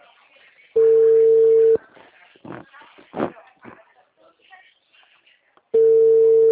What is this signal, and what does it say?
Ringback tone (China, 450 Hz), with speech.